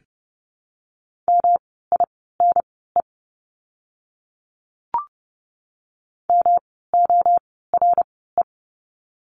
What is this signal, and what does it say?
Morse code, heard as a keyed tone, about 30 wpm, spaced as for 17 wpm, 700 Hz. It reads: MINE MORE
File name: course / 30:10wpm / 108 Lesson 18f - Letter m - Words 30wpm